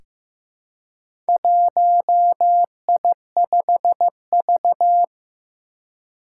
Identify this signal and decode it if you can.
Morse code, heard as a keyed tone, about 15 wpm, 700 Hz. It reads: 1I5V
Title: Four Challenging Letter-Number Combinations 15wpm